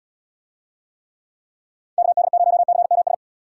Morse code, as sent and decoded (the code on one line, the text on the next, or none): -.- -.. ----- -.-. -- ...
KD0CMS